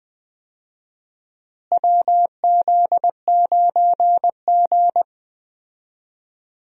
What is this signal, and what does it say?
Morse code, heard as a keyed tone, about 20 wpm, 700 Hz. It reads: WZ9G
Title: USA Call Signs - 4 Characters - Repeated 3x - 20wpm